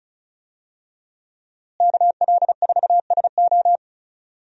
Morse code, heard as a keyed tone, about 34 wpm, 700 Hz. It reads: KL4SO
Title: USA Call Signs - 5 Characters - Repeated 3x - 35wpm